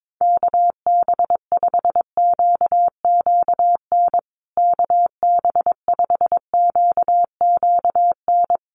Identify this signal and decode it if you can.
Morse code, heard as a keyed tone, about 22 wpm, 700 Hz. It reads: KB5QQN KB5QQN
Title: USA Call Signs Repeated 3x 22wpm